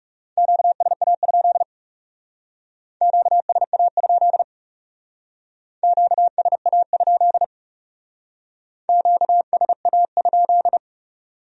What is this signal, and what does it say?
Morse code, heard as a keyed tone, about 38 wpm, 700 Hz. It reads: QSA? QSA? QSA? QSA?